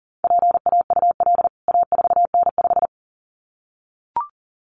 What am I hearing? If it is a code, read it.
Morse code, heard as a keyed tone, about 40 wpm, 700 Hz. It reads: PAUL A4N5
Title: CWT Exchange 40wpm